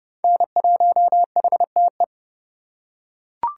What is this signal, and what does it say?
Morse code, heard as a keyed tone, about 28 wpm, 700 Hz. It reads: N1HTE